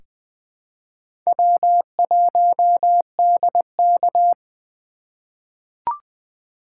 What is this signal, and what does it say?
Morse code, heard as a keyed tone, about 21 wpm, 700 Hz. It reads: W1DK
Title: USA Call Signs - 3x Character Spacing - 20wpm